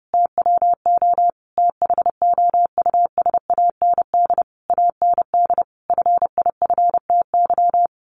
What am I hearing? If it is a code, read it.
Morse code, heard as a keyed tone, about 30 wpm, 700 Hz. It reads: TWO THOUSAND AND FIFTY
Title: A Sound Of Thunder 30wpm